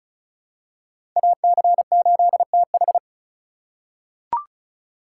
Morse code, heard as a keyed tone, about 36 wpm, 700 Hz. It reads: AC8TH
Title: USA Call Signs - 5 Characters - 3x Character Spacing - 35wpm